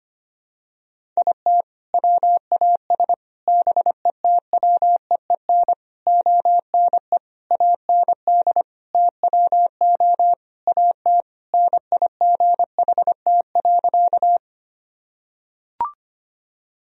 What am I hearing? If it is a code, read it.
Morse code, heard as a keyed tone, about 25 wpm, 700 Hz. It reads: IT WAS BETWEEN ONE AND TWO AT NIGHT.
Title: Sentences from Top 1000 Words - Review Part 1 - 25wpm